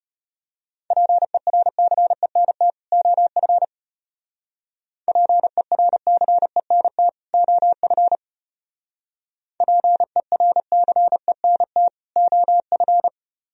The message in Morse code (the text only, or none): PERCENT OF PERCENT OF PERCENT OF